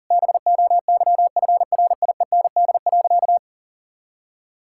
Morse code, heard as a keyed tone, about 40 wpm, 700 Hz. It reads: BOYFRIEND.
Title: Sentences from Top 1000 Words - Review Part 6 - 40wpm